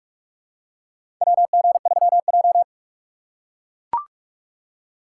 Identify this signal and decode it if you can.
Morse code, heard as a keyed tone, about 44 wpm, 700 Hz. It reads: WG3J
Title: USA Call Signs - 4 Characters - 6x Character Spacing - 45wpm